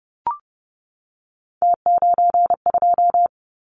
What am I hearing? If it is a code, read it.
Morse code, heard as a keyed tone, about 30 wpm, 700 Hz. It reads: T92
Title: Callsign Prefixes for Top 50 DXCC - 30wpm